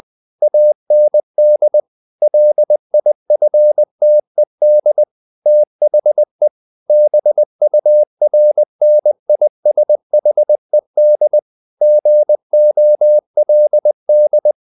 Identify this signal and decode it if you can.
Morse code, heard as a keyed tone, about 20 wpm, 600 Hz. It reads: AND LIFTED THE BURNISHED GOLD
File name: the old man and the sea part 2 20wpm